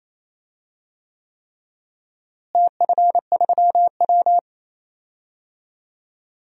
Morse code, heard as a keyed tone, about 28 wpm, 700 Hz. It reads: TF3W
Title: DXCC-100 Call Signs - 4 Characters - Repeated 3x - 28wpm